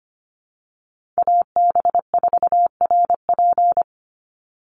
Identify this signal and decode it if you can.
Morse code, heard as a keyed tone, about 25 wpm, 700 Hz. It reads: AB4RP
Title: USA Call Signs 25wpm